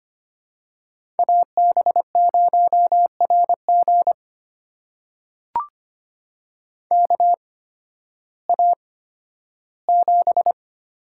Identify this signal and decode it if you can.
Morse code, heard as a keyed tone, about 25 wpm, spaced as for 13 wpm, 700 Hz. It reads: AB0RG KA7
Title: USA Call Signs - 8x Character Spacing - 25wpm